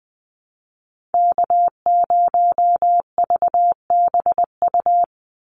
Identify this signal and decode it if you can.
Morse code, heard as a keyed tone, about 20 wpm, 700 Hz. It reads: K0VBU
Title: Call Signs - 20wpm